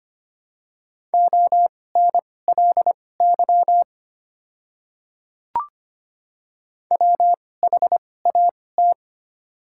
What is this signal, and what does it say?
Morse code, heard as a keyed tone, about 25 wpm, spaced as for 18 wpm, 700 Hz. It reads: ONLY WHAT
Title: 2x Character Spacing - Top 300 Words - 25wpm